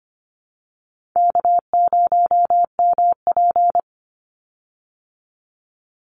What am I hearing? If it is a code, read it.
Morse code, heard as a keyed tone, about 25 wpm, 700 Hz. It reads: K0MP
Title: Easy Call Signs Repeated 3x - 25wpm